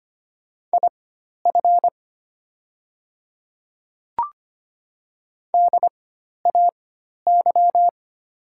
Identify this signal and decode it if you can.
Morse code, heard as a keyed tone, about 25 wpm, spaced as for 12 wpm, 700 Hz. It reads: IF DAY
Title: 4x Character Spacing - Top 100 Words - 25wpm